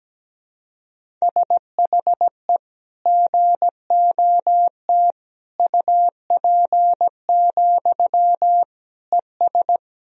Morse code, heard as a keyed tone, about 17 wpm, 700 Hz. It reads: SHE GOT UP, ES
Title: Sentences from Top 500 Words and Abbreviations 17wpm